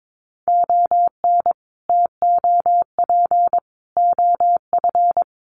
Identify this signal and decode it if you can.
Morse code, heard as a keyed tone, about 22 wpm, 700 Hz. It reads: ON TOP OF
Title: Sets of 3 Words 22wpm